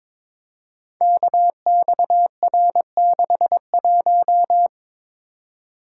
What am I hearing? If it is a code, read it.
Morse code, heard as a keyed tone, about 22 wpm, 700 Hz. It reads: KXR61
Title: Code Groups - 5 Letter:Numbers - 22wpm